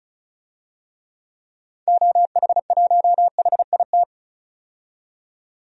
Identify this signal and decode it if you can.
Morse code, heard as a keyed tone, about 36 wpm, 690 Hz. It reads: OH1HIT
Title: DXCC-100 Call Signs - 6 Characters - 35wpm